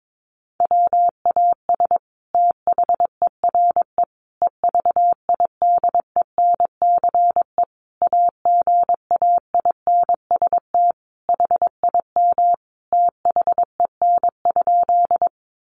Morse code, heard as a keyed tone, about 22 wpm, 700 Hz. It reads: WAS THERE EVIDENCE AGAINST HIM THEN?
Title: Sentences from Top 1300 Words 22wpm